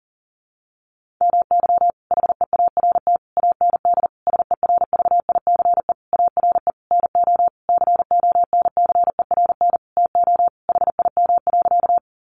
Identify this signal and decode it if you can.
Morse code, heard as a keyed tone, about 40 wpm, 700 Hz. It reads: MY HEART AND SERVICE ARE NO CONCERN TO HIM.
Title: Sentences from Top 1200 Words 40wpm